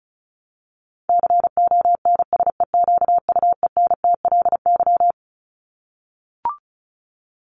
Morse code, heard as a keyed tone, about 34 wpm, 700 Hz. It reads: CONSEQUENTLY
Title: LY Suffix - Top 100 Words with Base Words 35wpm